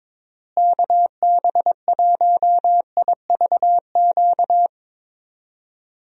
Morse code, heard as a keyed tone, about 22 wpm, 700 Hz. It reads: KB1IVQ
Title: USA Call Signs - 6 Characters - Repeated 3x - 22wpm